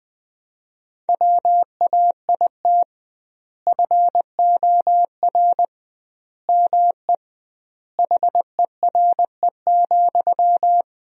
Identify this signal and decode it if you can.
Morse code, heard as a keyed tone, about 20 wpm, 700 Hz. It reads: WAIT FOR ME HERE,
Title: 2x Word-Spacing - Sentences from Top 500 Words - 20wpm